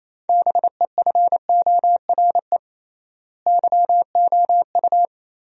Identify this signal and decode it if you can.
Morse code, heard as a keyed tone, about 28 wpm, 700 Hz. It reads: BEFORE YOU
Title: 3x Word-Spacing - Sentences from Top 400 Words - 28wpm